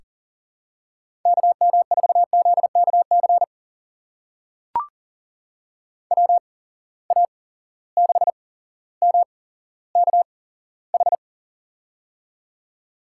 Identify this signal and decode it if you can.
Morse code, heard as a keyed tone, about 40 wpm, spaced as for 17 wpm, 700 Hz. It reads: KM4ZKC WA6MKH
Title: USA Call Signs - 6 Characters - 8x Character Spacing - 40wpm